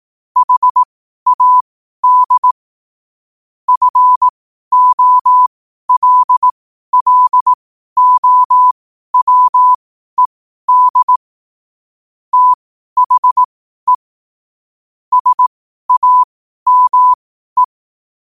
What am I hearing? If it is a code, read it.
Morse code, heard as a keyed tone, about 18 wpm, spaced as for 12 wpm, 1000 Hz. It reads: HAD FOLLOWED THE SAME
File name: qotdfiles / 12wpm